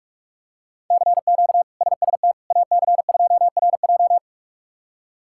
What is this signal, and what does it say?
Morse code, heard as a keyed tone, about 46 wpm, 700 Hz. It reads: CQ SST AC2RJ
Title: K1USN SST QSO Practice - 45wpm